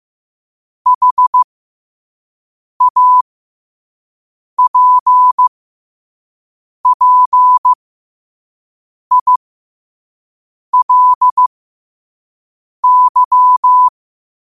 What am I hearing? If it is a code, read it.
Morse code, heard as a keyed tone, about 15 wpm, spaced as for 5 wpm, 1000 Hz. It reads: HAPPILY
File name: qotdfiles / 5wpm